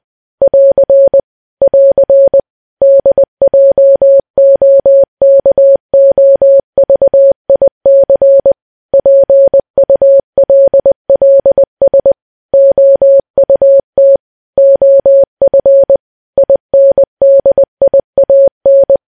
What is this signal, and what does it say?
Morse code, heard as a keyed tone, about 20 wpm, 550 Hz. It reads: <AR> <AR> DJOKOVIC PULLS OUT OF INDIAN
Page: news-headlines-in-morse-code-at-20-wpm